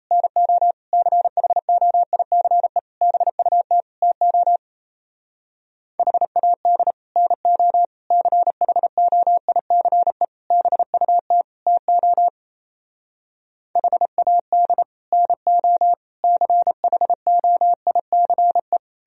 Morse code, heard as a keyed tone, about 37 wpm, 700 Hz. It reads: NO CHOICE BUT TO HAD NO CHOICE BUT TO HAD NO CHOICE